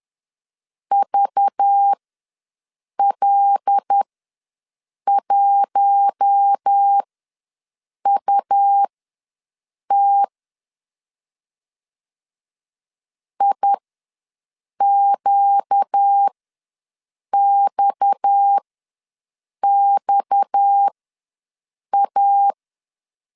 Morse code, heard as a keyed tone, about 10 wpm, spaced as for 6 wpm, 800 Hz. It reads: VL1UT IQXXA
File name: LEZIONI / 6 PKX0 / 6 PKX ws 10wpm 3706